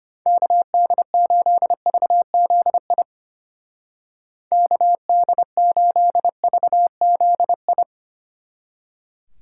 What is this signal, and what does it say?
Morse code, heard as a keyed tone, about 29 wpm, 700 Hz. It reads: KD8VZI KD8VZI
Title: Speed Racing - USA 2x3 Call Signs 25wpm